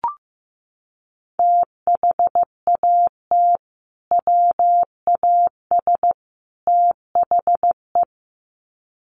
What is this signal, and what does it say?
Morse code, heard as a keyed tone, about 15 wpm, 700 Hz. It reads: THAT WAS THE
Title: Sets of 3 Words 15wpm